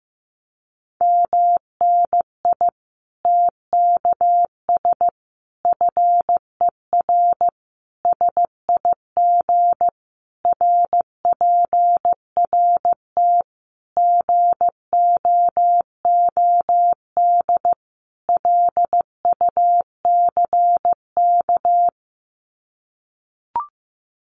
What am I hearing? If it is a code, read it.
Morse code, heard as a keyed tone, about 15 wpm, 700 Hz. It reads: MNI TKS FER SIG RPRT GOOD LUCK
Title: Common QSO Phrases 15wpm